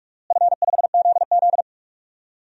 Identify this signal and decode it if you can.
Morse code, heard as a keyed tone, about 46 wpm, 700 Hz. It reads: F5ZZ
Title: DXCC-100 Call Signs - 4 Characters - Repeated 3x - 45wpm